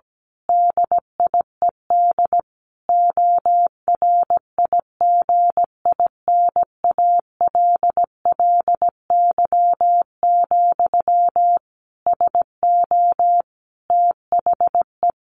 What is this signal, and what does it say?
Morse code, heard as a keyed tone, about 17 wpm, 700 Hz. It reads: DIED ORIGINALLY, SO THE
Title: A Sound Of Thunder 17wpm